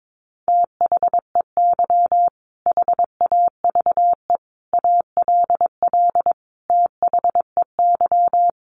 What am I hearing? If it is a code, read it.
Morse code, heard as a keyed tone, about 22 wpm, 700 Hz. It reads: THEY HAVE ALL THEY
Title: Sentences from Top 1100 Words 22wpm